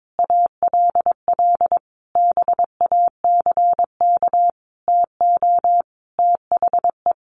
Morse code, heard as a keyed tone, about 22 wpm, 700 Hz. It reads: ALL BACK TO THE
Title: Sentences from Top 1500 Words 22wpm